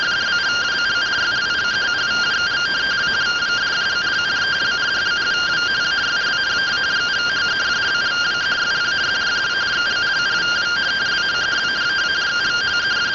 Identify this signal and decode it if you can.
Two-tone teleprinter signal, about 170 Hz shift.
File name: findings / 202 coronal iodine 3mm.gen-13 rtty rx